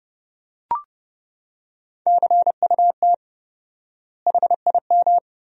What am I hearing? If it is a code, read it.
Morse code, heard as a keyed tone, about 32 wpm, 700 Hz. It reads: CUT HIM
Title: 4x Word-Spacing - Sentences from Top 300 Words - 30wpm